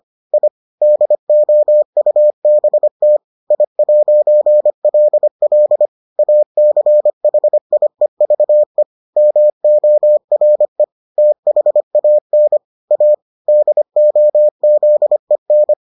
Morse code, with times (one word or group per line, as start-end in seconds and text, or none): 0.3-0.5 I
0.8-3.2 DOUBT
3.5-5.9 I'LL
6.2-8.8 ACHIEVE
9.2-10.8 MORE
11.2-12.6 THAN
12.9-13.1 A
13.5-15.7 DOZEN